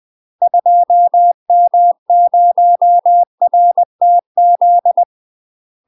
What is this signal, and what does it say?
Morse code, heard as a keyed tone, about 20 wpm, 700 Hz. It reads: 2M0RTZ
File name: UK MASTER QSOs TRAINING / 20WPM / UK MASTER QSOs 0063